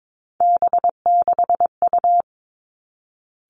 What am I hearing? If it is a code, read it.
Morse code, heard as a keyed tone, about 22 wpm, 700 Hz. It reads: B6U
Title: Three Challenging Letter-Number Combinations 22wpm